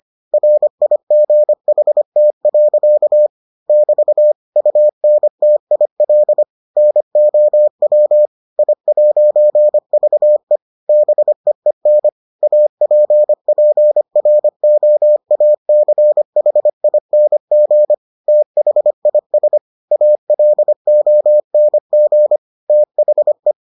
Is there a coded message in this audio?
Yes: Morse code from 0.3 to 23.5 s, heard as a keyed tone, about 25 wpm, 600 Hz. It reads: RIGHT. <BT> UNTIL NOW I'VE BEEN APPROACHING THIS ALONG THE